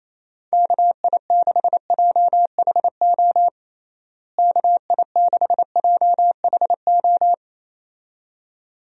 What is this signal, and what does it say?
Morse code, heard as a keyed tone, about 28 wpm, 700 Hz. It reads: KI6JHO KI6JHO
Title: USA Call Signs - 6 Characters - Repeated 2x - 28wpm